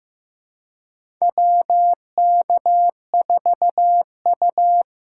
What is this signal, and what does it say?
Morse code, heard as a keyed tone, about 15 wpm, 700 Hz. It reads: WK4U
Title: USA Call Signs - 4 Characters - 4x Character Spacing - 15wpm